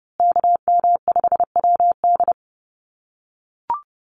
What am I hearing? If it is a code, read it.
Morse code, heard as a keyed tone, about 30 wpm, 700 Hz. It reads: KM5WD
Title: USA Call Signs Repeated 2x 30wpm